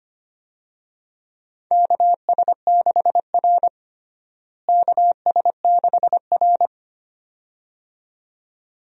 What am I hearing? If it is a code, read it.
Morse code, heard as a keyed tone, about 25 wpm, 700 Hz. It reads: KS6R KS6R
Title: USA Call Signs - 4 Characters - Repeated 2x - 25wpm